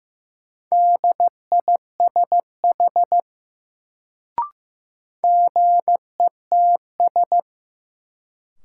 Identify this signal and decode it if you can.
Morse code, heard as a keyed tone, about 15 wpm, 700 Hz. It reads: DISH GETS
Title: Sentences from Top 1300 Words - New Word Drill - 15wpm